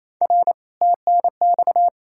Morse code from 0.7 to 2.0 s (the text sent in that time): TNX